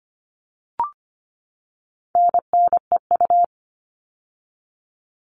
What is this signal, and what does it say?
Morse code, heard as a keyed tone, about 25 wpm, 700 Hz. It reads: NNEU
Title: All Cut Numbers - 4-Digit 25wpm